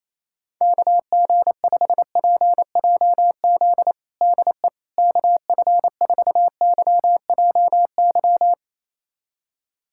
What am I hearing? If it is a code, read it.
Morse code, heard as a keyed tone, about 28 wpm, 700 Hz. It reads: KG5PJZ DE KF4YJY
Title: USA Callsign DE Callsign - 6 Characters - 28wpm